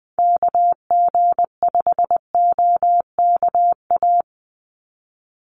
Morse code, heard as a keyed tone, about 20 wpm, 700 Hz. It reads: KG5OKA
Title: USA Call Signs 20wpm